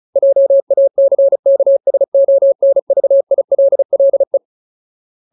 Morse code, heard as a keyed tone, about 35 wpm, 550 Hz. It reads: JACKSONVILLE
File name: Ultimate Word List / 35WPM Ultimate